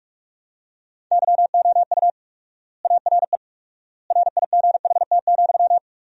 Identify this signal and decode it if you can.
Morse code, heard as a keyed tone, about 46 wpm, 700 Hz. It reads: YOU ARE RIGHT,
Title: 4x Word-Spacing - Sentences from Top 300 Words - 45wpm